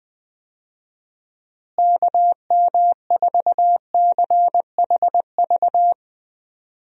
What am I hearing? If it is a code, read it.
Morse code, heard as a keyed tone, about 20 wpm, 700 Hz. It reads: KM4CHV
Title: USA Call Signs - 6 Characters - Repeated 2x - 20wpm